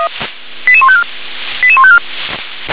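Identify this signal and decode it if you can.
Five-tone selective call (ZVEI), heard as quick runs of five tones, tones of about 70 ms. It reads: 88154 88154
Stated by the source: ZVEI five-tone selective call received on an SDR.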